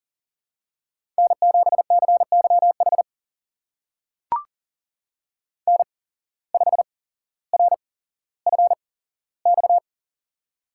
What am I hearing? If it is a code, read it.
Morse code, heard as a keyed tone, about 40 wpm, spaced as for 17 wpm, 700 Hz. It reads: N7CYH N5RFX
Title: USA Call Signs - 5 Characters - 8x Character Spacing - 40wpm